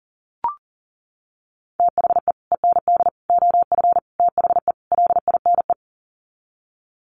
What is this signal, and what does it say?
Morse code, heard as a keyed tone, about 40 wpm, 700 Hz. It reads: THE END OF THE LINE
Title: Sets of 5 Words Encore2 40wpm